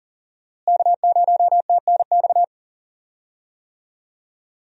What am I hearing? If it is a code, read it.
Morse code, heard as a keyed tone, about 38 wpm, 700 Hz. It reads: K0TNX
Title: USA Call Signs - 5 Characters - Repeated 3x - 40wpm